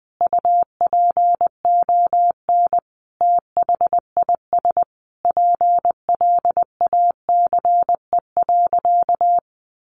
Morse code, heard as a keyed tone, about 19 wpm, 700 Hz. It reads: UPON THIS PLACE.
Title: Sentences from Top 1100 Words 20wpm